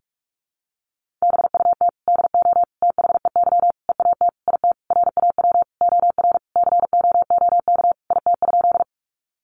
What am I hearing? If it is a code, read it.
Morse code, heard as a keyed tone, about 44 wpm, 700 Hz. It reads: BUT DO THEY EAT IT RAW OR COOK IT?
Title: Sentences from Top 1500 Words 45wpm